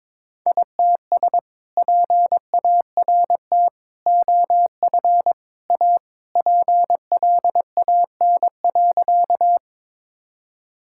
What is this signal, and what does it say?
Morse code, heard as a keyed tone, about 22 wpm, 700 Hz. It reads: ITS PART OF A PLAN.
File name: practice / 22wpm / Sentences with 2-Word Phrases - 22wpm